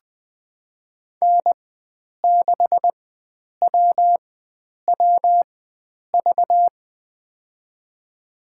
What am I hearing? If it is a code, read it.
Morse code, heard as a keyed tone, about 20 wpm, spaced as for 9 wpm, 700 Hz. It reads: N6WWV
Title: USA Call Signs - 5 Characters - 4x Character Spacing - 20wpm